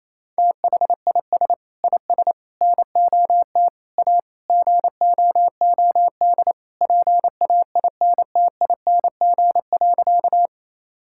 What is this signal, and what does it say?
Morse code, heard as a keyed tone, about 27 wpm, 700 Hz. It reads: THIS IS NOT A GOOD PAINTING.
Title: Sentences from Top 1000 Words - Review Part 6 - 28wpm